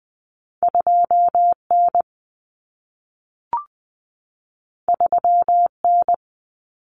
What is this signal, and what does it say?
Morse code, heard as a keyed tone, about 20 wpm, 700 Hz. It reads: 2N 3N
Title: Common Cut Numbers - 2-Digit 20wpm